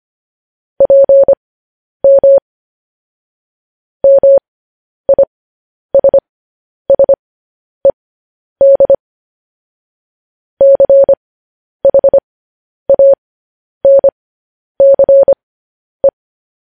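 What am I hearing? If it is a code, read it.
Morse code, heard as a keyed tone, about 25 wpm, spaced as for 10 wpm, 550 Hz. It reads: PM MISSED CHANCE
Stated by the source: Morse code news headlines sent at 10 wpm for practice.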